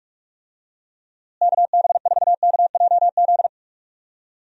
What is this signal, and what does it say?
Morse code, heard as a keyed tone, about 44 wpm, 700 Hz. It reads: KB4KJZ
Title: USA Call Signs - 6 Characters - Repeated 3x - 45wpm